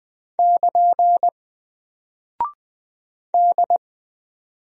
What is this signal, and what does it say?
Morse code, heard as a keyed tone, about 20 wpm, 700 Hz. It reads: <KN> D
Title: Single Character (Letter:Number:Punctuation:Prosign) 20wpm